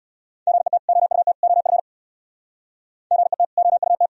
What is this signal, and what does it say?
Morse code, heard as a keyed tone, about 76 wpm, 700 Hz. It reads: GET OUT OF GET OUT